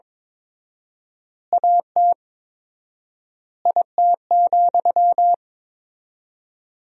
Morse code, heard as a keyed tone, about 22 wpm, 700 Hz. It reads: AT IT,
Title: 4x Word-Spacing - Sentences from Top 200 Words - 22wpm